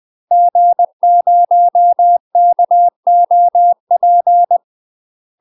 Morse code, heard as a keyed tone, about 20 wpm, 700 Hz. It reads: G0KOP
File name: UK MASTER QSOs TRAINING / 20WPM / UK MASTER QSOs 0035